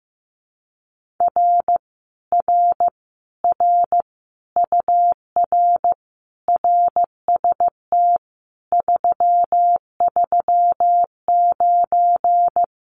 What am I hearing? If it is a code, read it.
Morse code, heard as a keyed tone, about 15 wpm, 700 Hz. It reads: R R R UR RST 339
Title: Common QSO Phrases Encore 15wpm